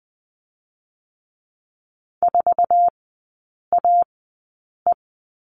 Morse code, heard as a keyed tone, about 20 wpm, spaced as for 8 wpm, 700 Hz. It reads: 4AE
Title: 047 Lesson 9e - Number 4 - Three Characters 20wpm